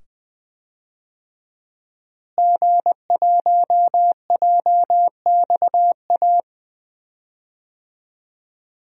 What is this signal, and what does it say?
Morse code, heard as a keyed tone, about 20 wpm, 700 Hz. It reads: G1JXA
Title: DXCC-100 Call Signs - 5 Characters - Repeated 2x - 20wpm